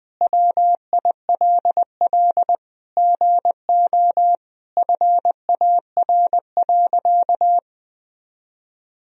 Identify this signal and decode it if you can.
Morse code, heard as a keyed tone, about 20 wpm, 700 Hz. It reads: WILL GO FAR.
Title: Sentences from Top 400 Words and Abbreviations 20wpm